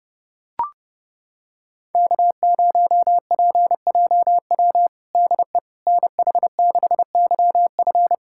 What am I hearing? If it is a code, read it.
Morse code, heard as a keyed tone, about 30 wpm, 700 Hz. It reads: K0PJW DE NH6YF
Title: USA Callsign DE Callsign - 5 Characters - 30wpm